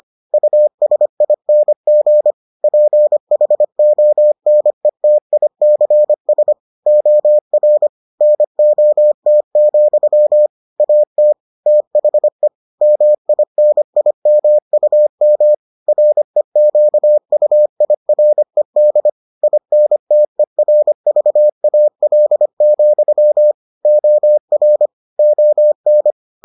Morse code, heard as a keyed tone, about 25 wpm, 600 Hz. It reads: USING PHONETICS OR NOT, AT THE MINIMUM REQUIRED INTERVAL, OR ON